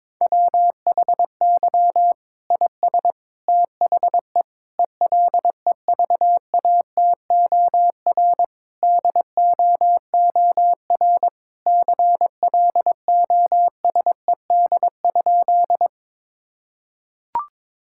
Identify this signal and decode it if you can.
Morse code, heard as a keyed tone, about 22 wpm, 700 Hz. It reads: WHY IS THE ELEVATOR DOOR CLOSED?
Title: Sentences from Top 1600 Words 22wpm